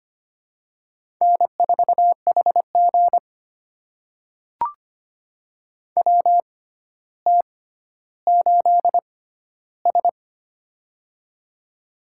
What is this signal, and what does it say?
Morse code, heard as a keyed tone, about 25 wpm, spaced as for 13 wpm, 700 Hz. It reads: N4HG WT8S
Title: USA Call Signs - 4 Characters - 6x Character Spacing - 25wpm